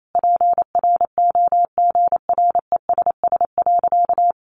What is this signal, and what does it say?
Morse code, heard as a keyed tone, about 28 wpm, 700 Hz. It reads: PROGRESS.
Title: Sentences from Top 1400 Words 28wpm